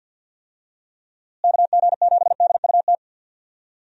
Morse code, heard as a keyed tone, about 52 wpm, 700 Hz. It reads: KG7DUT
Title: USA Call Signs - 6 Characters - Repeated 3x - 50wpm